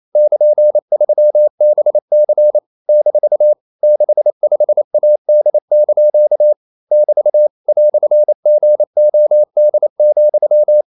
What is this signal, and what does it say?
Morse code, heard as a keyed tone, about 28 wpm, 600 Hz. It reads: <KN>3BC - 65AD) <BT> "GOD,
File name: CW / Practice / EnglishQuotations0228